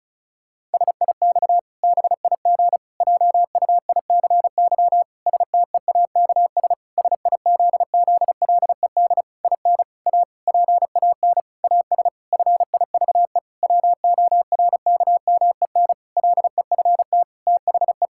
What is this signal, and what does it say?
Morse code, heard as a keyed tone, about 36 wpm, 700 Hz. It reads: SIX BIG JUICY STEAKS SIZZLED IN A PAN AS FIVE WORKMEN LEFT THE
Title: Pangram Challenge v2 - 35wpm